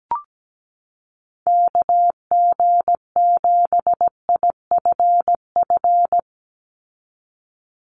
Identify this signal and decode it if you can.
Morse code, heard as a keyed tone, about 17 wpm, 700 Hz. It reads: KG7IFF
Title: USA Call Signs 17wpm